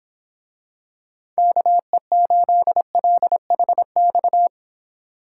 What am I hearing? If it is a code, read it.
Morse code, heard as a keyed tone, about 25 wpm, 700 Hz. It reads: KE8LHX